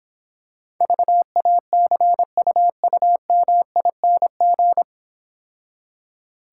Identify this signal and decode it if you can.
Morse code, heard as a keyed tone, about 26 wpm, 700 Hz. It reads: VACUUMING